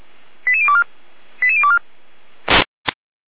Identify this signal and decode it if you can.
Five-tone selective call (ZVEI), heard as quick runs of five tones, tones of about 70 ms. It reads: 88024 88024
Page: SDR/ZVEI